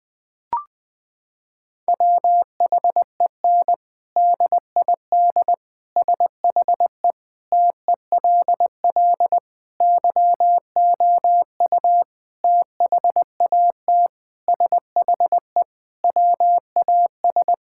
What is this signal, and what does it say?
Morse code, heard as a keyed tone, about 19 wpm, 700 Hz. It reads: WHEN DID SHE TELL YOU THAT SHE WAS
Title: Sentences from Top 1000 Words - Review Part 6 - 20wpm